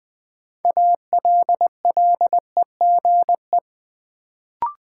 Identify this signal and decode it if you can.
Morse code, heard as a keyed tone, about 20 wpm, 700 Hz. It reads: ALLEGE
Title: Build A Word - 3 Letters At A Time - 6-Letter Words - 20wpm